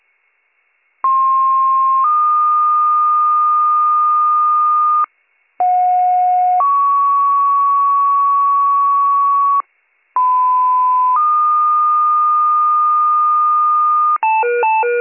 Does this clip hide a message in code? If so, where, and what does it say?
Yes: two-tone page from 1.0 to 14.2 s, heard as a one-second tone, then a three-second tone, about 1060 then 1220 Hz, about 730 then 1120 Hz, about 1010 then 1220 Hz.